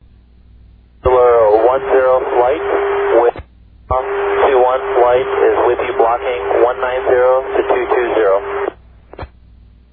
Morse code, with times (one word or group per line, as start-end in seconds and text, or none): none